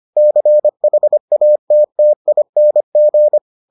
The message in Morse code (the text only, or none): CHATTING